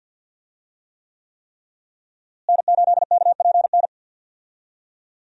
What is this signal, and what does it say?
Morse code, heard as a keyed tone, about 51 wpm, 700 Hz. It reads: N8KPN
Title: USA Call Signs - 5 Characters - Repeated 3x - 50wpm